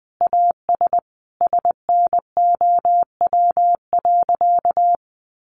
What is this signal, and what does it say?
Morse code, heard as a keyed tone, about 20 wpm, 700 Hz. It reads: AS SNOW.
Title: Sentences from Top 1500 Words 20wpm